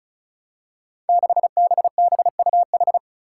Morse code, heard as a keyed tone, about 36 wpm, 700 Hz. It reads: 6BBUH
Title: Five Challenging Letter-Number Combinations 35wpm